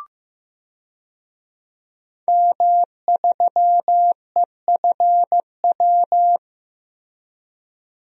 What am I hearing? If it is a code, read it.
Morse code, heard as a keyed tone, about 15 wpm, 700 Hz. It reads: M3EFW
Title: DXCC-100 Call Signs - 5 Characters - Repeated 2x - 15wpm